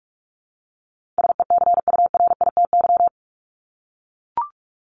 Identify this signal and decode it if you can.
Morse code, heard as a keyed tone, about 44 wpm, 700 Hz. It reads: SECURITY
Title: Top 501-750 Words 45wpm